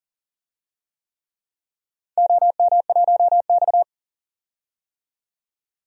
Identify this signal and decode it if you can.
Morse code, heard as a keyed tone, about 40 wpm, 700 Hz. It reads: OM1X
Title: DXCC-100 Call Signs - 4 Characters - 40wpm